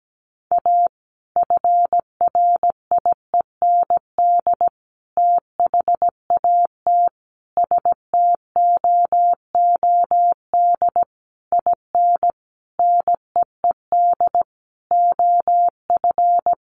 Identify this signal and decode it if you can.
Morse code, heard as a keyed tone, about 17 wpm, 700 Hz. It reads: A FRIEND THAT STOOD IN NEED OF
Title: Sentences from Top 1500 Words 17wpm